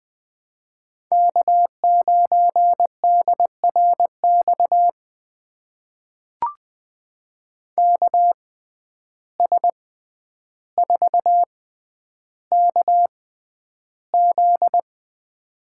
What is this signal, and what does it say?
Morse code, heard as a keyed tone, about 20 wpm, spaced as for 10 wpm, 700 Hz. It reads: K9DRX KS4KZ
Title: USA Call Signs - 6x Character Spacing - 20wpm